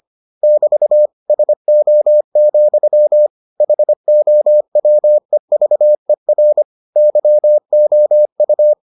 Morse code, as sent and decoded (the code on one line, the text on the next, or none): -...- / ... --- --..-- / .... --- .-- . ...- . .-. / -.-- --- ..-
<BT> SO, HOWEVER YOU